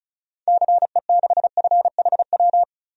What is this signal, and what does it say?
Morse code, heard as a keyed tone, about 34 wpm, 700 Hz. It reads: CE6FHW
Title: DXCC-100 Call Signs - 6 Characters - 35wpm